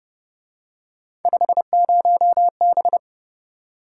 Morse code, heard as a keyed tone, about 30 wpm, 700 Hz. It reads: 50B
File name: Speed-Racing - Three Letter-Number Combinations - 25wpm